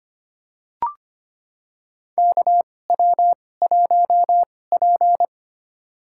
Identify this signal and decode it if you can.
Morse code, heard as a keyed tone, about 25 wpm, spaced as for 18 wpm, 700 Hz. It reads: KW1P
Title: USA Call Signs - 4 Characters - 2x Character Spacing - 25wpm